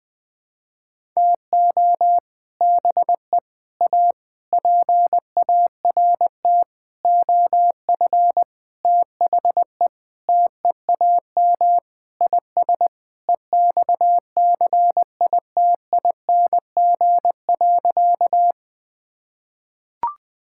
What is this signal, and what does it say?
Morse code, heard as a keyed tone, about 20 wpm, 700 Hz. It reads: TO BE A PART OF THE TEAM IS EXCITING.
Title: Sentences with 6-Word Phrases - Spoken Beginning - 20wpm